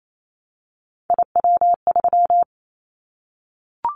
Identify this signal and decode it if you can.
Morse code, heard as a keyed tone, about 28 wpm, 700 Hz. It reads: IW3
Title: Callsign Prefixes for Top 50 DXCC - 28wpm